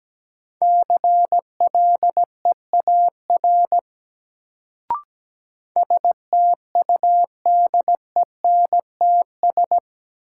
Morse code, heard as a keyed tone, about 17 wpm, 700 Hz. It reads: CLEAR STUDENTS
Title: Sets of 5 Words Encore - New Word Drill - 17wpm